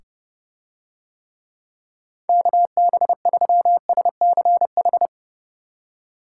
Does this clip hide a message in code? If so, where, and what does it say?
Yes: Morse code from 2.3 to 5.1 s, heard as a keyed tone, about 26 wpm, 700 Hz. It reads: KB3SCH